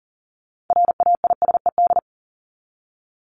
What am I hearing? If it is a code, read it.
Morse code, heard as a keyed tone, about 40 wpm, 700 Hz. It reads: RAISED
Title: ED Suffix - Top 100 Words with Base Words 40wpm